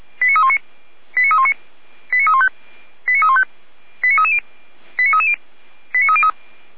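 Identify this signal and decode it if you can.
Five-tone selective call (ZVEI), heard as quick runs of five tones, tones of about 70 ms. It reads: 89319 89319 89315 89315 89330 89330 89303
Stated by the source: ZVEI five-tone selective call received on SDR.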